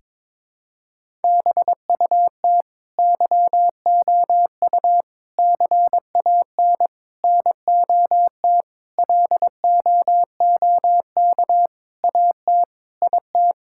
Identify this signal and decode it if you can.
Morse code, heard as a keyed tone, about 22 wpm, 700 Hz. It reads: BUT YOU CAN NOT LOOK AT IT
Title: Sentences from Top 1000 Words - Review Part 6 - 22wpm